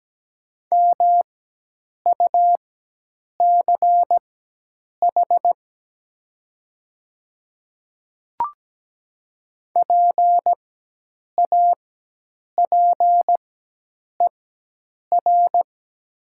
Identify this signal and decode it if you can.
Morse code, heard as a keyed tone, about 17 wpm, spaced as for 8 wpm, 700 Hz. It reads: MUCH PAPER
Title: 4x Character Spacing - Top 300 Words - 17wpm